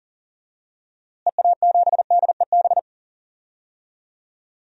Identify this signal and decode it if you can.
Morse code, heard as a keyed tone, about 37 wpm, 700 Hz. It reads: EA7DEB
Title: DXCC-100 Call Signs - 6 Characters - Repeated 2x - 40wpm